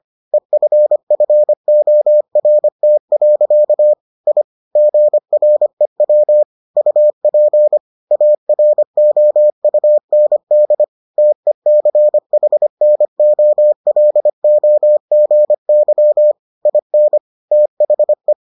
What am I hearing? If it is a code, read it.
Morse code, heard as a keyed tone, about 25 wpm, 600 Hz. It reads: EFFORT. I GREW UP AROUND TECHNOLOGY IN THE